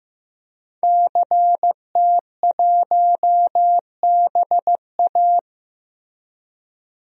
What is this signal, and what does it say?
Morse code, heard as a keyed tone, about 15 wpm, 700 Hz. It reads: CT1BA
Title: DXCC-100 Call Signs - 5 Characters - Repeated 3x - 15wpm